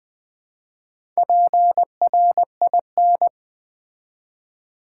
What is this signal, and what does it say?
Morse code, heard as a keyed tone, about 20 wpm, 700 Hz. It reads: PRIN
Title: Build A Word - 1 Letter At A Time - 5-Letter Words - 20wpm